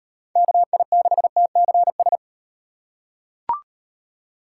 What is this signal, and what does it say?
Morse code, heard as a keyed tone, about 38 wpm, 700 Hz. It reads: KI6TCS